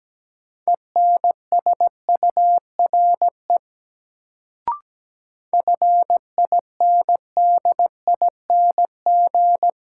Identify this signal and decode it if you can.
Morse code, heard as a keyed tone, about 17 wpm, 700 Hz. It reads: ENSURE FINDING
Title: Sentences from Top 1100 Words - New Word Drill - 17wpm